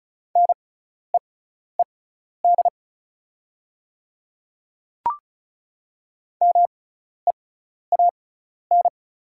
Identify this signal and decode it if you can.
Morse code, heard as a keyed tone, about 36 wpm, spaced as for 12 wpm, 700 Hz. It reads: NEED MEAN
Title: 6x Character Spacing - Top 200 Words - 35wpm